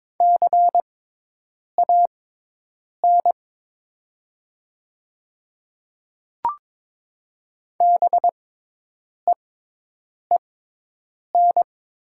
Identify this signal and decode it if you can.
Morse code, heard as a keyed tone, about 22 wpm, spaced as for 8 wpm, 700 Hz. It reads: CAN BEEN
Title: 6x Character Spacing - Top 100 Words - 22wpm